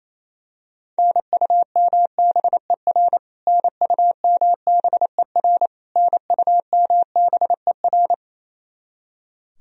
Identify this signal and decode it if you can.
Morse code, heard as a keyed tone, about 28 wpm, 700 Hz. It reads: NUMBER NUMBER NUMBER
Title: Top 100 Words Repeated 28wpm